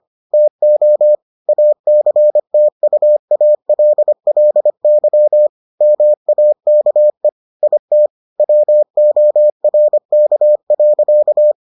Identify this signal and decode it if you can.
Morse code, heard as a keyed tone, about 25 wpm, 600 Hz. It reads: TO ACTUALLY MAKE IT WORK.